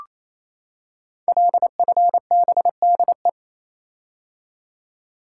Morse code, heard as a keyed tone, about 28 wpm, 700 Hz. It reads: LFBDE
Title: Code Groups - 5 Letters - 28wpm